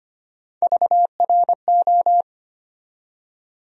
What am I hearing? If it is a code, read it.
Morse code, heard as a keyed tone, about 25 wpm, 700 Hz. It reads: VRO
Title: USA Call Sign Postfixes - 3 Letters 25wpm